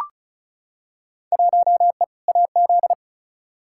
Morse code, heard as a keyed tone, about 34 wpm, 700 Hz. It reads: 1E AZ
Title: Field Day Exchanges - 35wpm